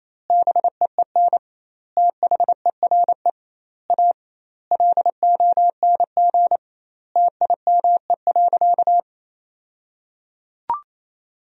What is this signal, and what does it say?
Morse code, heard as a keyed tone, about 27 wpm, 700 Hz. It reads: BEEN THERE A LONG TIME.
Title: 2x Word-Spacing - Sentences from Top 400 Words - 28wpm